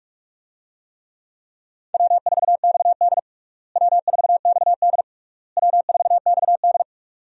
Morse code, heard as a keyed tone, about 46 wpm, 700 Hz. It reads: W4XD W4XD W4XD
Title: USA Call Signs - 4 Characters - Repeated 3x - 45wpm